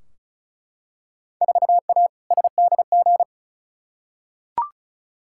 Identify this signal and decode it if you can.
Morse code, heard as a keyed tone, about 36 wpm, 700 Hz. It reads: 4A SDG
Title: Field Day Exchanges Encore - 35wpm